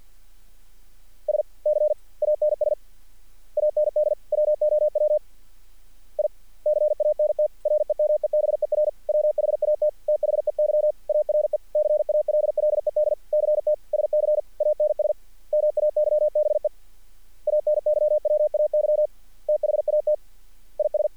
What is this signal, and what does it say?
Morse code, heard as a keyed tone, about 50 wpm, 600 Hz. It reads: SCANS AND WOW I CANT REMEBER WHAT THEY ARE CALLED CT SCANS MAYBE ANYWAY THAT IS
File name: archived / CRSnet